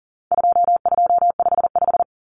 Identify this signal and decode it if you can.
Morse code, heard as a keyed tone, about 40 wpm, 700 Hz. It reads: 2255
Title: Numbers - 4-Digit 40wpm